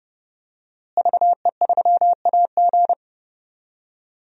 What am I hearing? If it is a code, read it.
Morse code, heard as a keyed tone, about 31 wpm, 700 Hz. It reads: VE3AG
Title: DXCC-100 Call Signs - 5 Characters - Repeated 2x - 30wpm